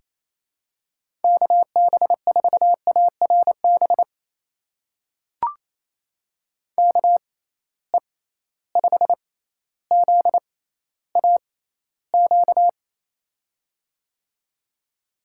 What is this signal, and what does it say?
Morse code, heard as a keyed tone, about 28 wpm, spaced as for 14 wpm, 700 Hz. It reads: KB4ARB KE5ZAQ
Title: USA Call Signs - 6x Character Spacing - 28wpm